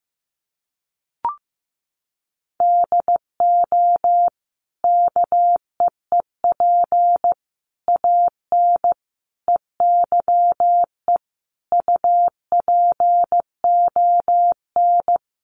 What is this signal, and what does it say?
Morse code, heard as a keyed tone, about 14 wpm, 700 Hz. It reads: DO KEEP AN EYE UPON
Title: Sentences from Top 1100 Words 15wpm